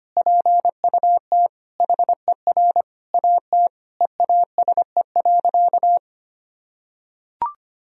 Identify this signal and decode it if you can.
Morse code, heard as a keyed tone, about 25 wpm, 700 Hz. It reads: PUT HER AT EASE.
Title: Sentences from Top 1000 Words - Review Part 1 - 25wpm